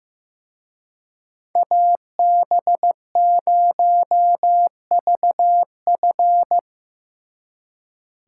Morse code, heard as a keyed tone, about 18 wpm, 700 Hz. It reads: AB0VF
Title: Speed Racing - USA 2x2 Call Signs 15wpm